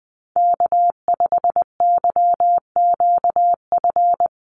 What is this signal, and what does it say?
Morse code, heard as a keyed tone, about 20 wpm, 700 Hz. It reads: K5YQF
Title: Call Signs - 20wpm